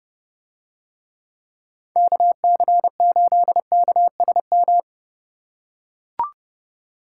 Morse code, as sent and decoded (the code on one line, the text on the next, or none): -.- -.-. ---.. -.- ... --
KC8KSM